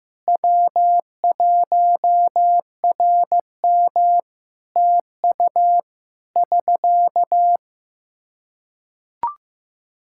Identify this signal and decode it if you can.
Morse code, heard as a keyed tone, about 15 wpm, 700 Hz. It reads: W1RM TU <SK>
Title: Practice QSO - Exchange 4 - Ending - 15wpm